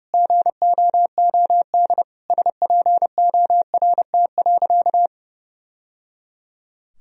Morse code, heard as a keyed tone, about 30 wpm, 700 Hz. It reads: GOOD SPORT.
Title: Sentences from Top 1000 Words - Review Part 4 - 30wpm